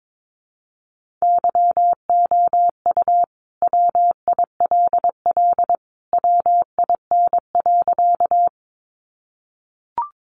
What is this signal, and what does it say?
Morse code, heard as a keyed tone, about 22 wpm, 700 Hz. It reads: YOU WILL WIN.
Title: Sentences from Top 1500 Words 22wpm